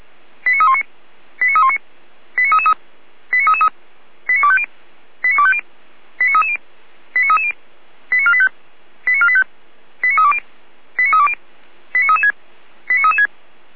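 Five-tone selective call (ZVEI), heard as quick runs of five tones, tones of about 80 ms. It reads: 89319 89319 89333 89333 89250 89250 89330 89330 89485 89485 89320 89320 89336 89336
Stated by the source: ZVEI five-tone selective call 89315 received by SDR, from the Feuerwehr Orlamünde audio server.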